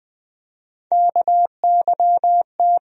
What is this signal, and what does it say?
Morse code, heard as a keyed tone, about 20 wpm, 700 Hz. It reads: KYT
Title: USA Call Sign Postfixes - 3 Letters 20wpm